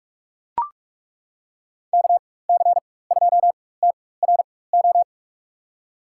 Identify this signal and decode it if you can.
Morse code, heard as a keyed tone, about 42 wpm, spaced as for 20 wpm, 700 Hz. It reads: KC2TRO
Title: USA Call Signs - 6 Characters - 4x Character Spacing - 45wpm